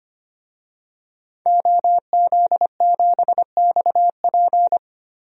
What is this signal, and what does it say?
Morse code, heard as a keyed tone, about 25 wpm, 700 Hz. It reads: OZ7XP
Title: DXCC-100 Call Signs - 5 Characters - Repeated 2x - 25wpm